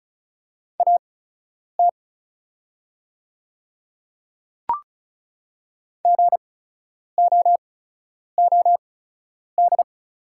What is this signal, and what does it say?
Morse code, heard as a keyed tone, about 34 wpm, spaced as for 10 wpm, 700 Hz. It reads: AT GOOD
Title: 8x Character Spacing - Top 300 Words - 35wpm